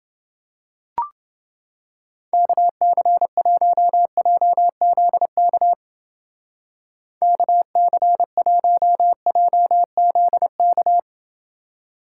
Morse code, heard as a keyed tone, about 28 wpm, 700 Hz. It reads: KC1JZK KC1JZK